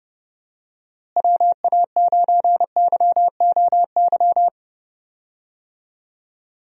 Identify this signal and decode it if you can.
Morse code, heard as a keyed tone, about 31 wpm, 700 Hz. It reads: WA9YOY
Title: USA Call Signs - 6 Characters - Repeated 2x - 30wpm